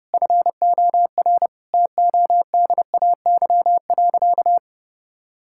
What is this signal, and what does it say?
Morse code, heard as a keyed tone, about 30 wpm, 700 Hz. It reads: FOR TODAY.
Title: Sentences from Top 1000 Words - Review Part 5 - 30wpm